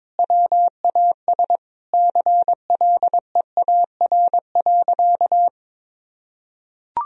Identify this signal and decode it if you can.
Morse code, heard as a keyed tone, about 22 wpm, 700 Hz. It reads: WAS CLEAR.
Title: Sentences from Top 1000 Words - Review Part 1 - 22wpm